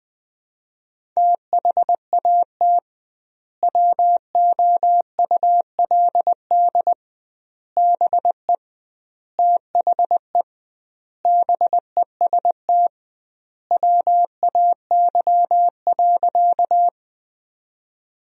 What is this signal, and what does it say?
Morse code, heard as a keyed tone, about 20 wpm, 700 Hz. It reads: THAT WOULD BE THE BEST WAY.
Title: 2x Word-Spacing - Sentences from Top 400 Words - 20wpm